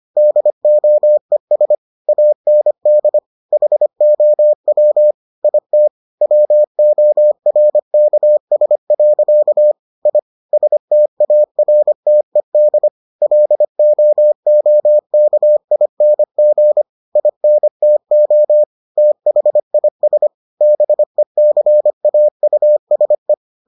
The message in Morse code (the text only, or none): DOES AND HOW IT WORKS. I STARTED LOOKING INTO THIS BECAUSE